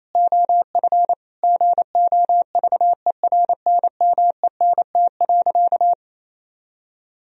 Morse code, heard as a keyed tone, about 28 wpm, 700 Hz. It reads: OF GOVERNMENT.
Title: Sentences from Top 1000 Words - Review Part 3 - 28wpm